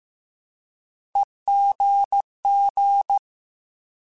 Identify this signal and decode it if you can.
Morse code, heard as a keyed tone, about 15 wpm, 800 Hz. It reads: EGG
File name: Alphabetical / Alphabetical mp3s / Es @13